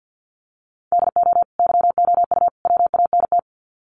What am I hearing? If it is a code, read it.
Morse code, heard as a keyed tone, about 50 wpm, 700 Hz. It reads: DO YOU WANT
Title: Sets of 3 Words 50wpm